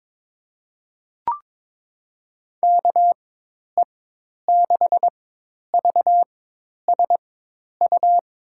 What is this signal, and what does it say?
Morse code, heard as a keyed tone, about 22 wpm, spaced as for 10 wpm, 700 Hz. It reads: KE6VSU
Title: USA Call Signs - 4x Character Spacing - 22wpm